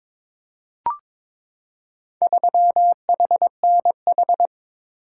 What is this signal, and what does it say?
Morse code, heard as a keyed tone, about 21 wpm, 700 Hz. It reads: 3HNH
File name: practice / 22wpm / Four Letter-Number Combinations - 22wpm